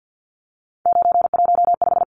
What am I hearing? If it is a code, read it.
Morse code, heard as a keyed tone, about 50 wpm, 700 Hz. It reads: 915
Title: Numbers - 3-Digit 50wpm